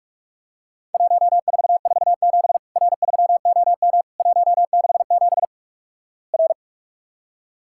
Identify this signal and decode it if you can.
Morse code, heard as a keyed tone, about 47 wpm, 700 Hz. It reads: 1447 R3OM 167 R
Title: Worked All Europe DX Contest - QTC Exchange - 45wpm